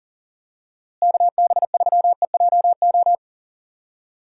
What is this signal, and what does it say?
Morse code, heard as a keyed tone, about 40 wpm, 700 Hz. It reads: KB3EJO
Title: USA Call Signs - 6 Characters - Repeated 3x - 40wpm